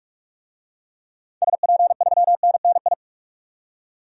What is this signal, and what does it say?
Morse code, heard as a keyed tone, about 41 wpm, 700 Hz. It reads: SP3NNI
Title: DXCC-100 Call Signs - 6 Characters - 45wpm